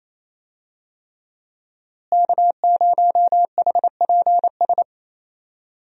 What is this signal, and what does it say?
Morse code, heard as a keyed tone, about 28 wpm, 700 Hz. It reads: K0HPS
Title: USA Call Signs - 5 Characters - Repeated 3x - 28wpm